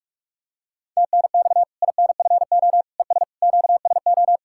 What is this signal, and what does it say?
Morse code, heard as a keyed tone, about 41 wpm, 700 Hz. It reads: TNX INFO ES QSO
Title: Practice QSO - Exchange 4 - Ending - 45wpm